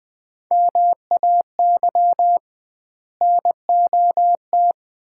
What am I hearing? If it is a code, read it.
Morse code, heard as a keyed tone, about 20 wpm, 700 Hz. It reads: MAY NOT
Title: 2x Word-Spacing - Sets of 5 Words - 20wpm